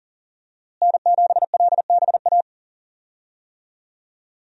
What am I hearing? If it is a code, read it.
Morse code, heard as a keyed tone, about 38 wpm, 700 Hz. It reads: N7LBA
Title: USA Call Signs - 5 Characters - Repeated 2x - 40wpm